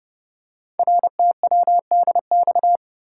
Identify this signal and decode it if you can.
Morse code, heard as a keyed tone, about 30 wpm, 700 Hz. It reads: RTWDX
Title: Code Groups - 5 Letters - 30wpm